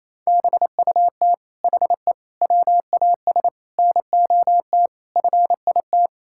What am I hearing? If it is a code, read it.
Morse code, heard as a keyed tone, about 28 wpm, 700 Hz. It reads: BUT HE WAS NOT FIT
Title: Sentences from Top 1000 Words - Review Part 6 - 28wpm